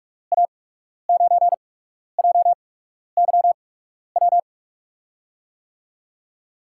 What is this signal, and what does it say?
Morse code, heard as a keyed tone, about 44 wpm, spaced as for 12 wpm, 700 Hz. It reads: A9JYW
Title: USA Call Signs - 6 Characters - 8x Character Spacing - 45wpm